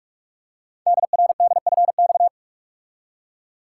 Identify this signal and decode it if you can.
Morse code, heard as a keyed tone, about 44 wpm, 700 Hz. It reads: DRDFX
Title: Code Groups - 5 Letters - 45wpm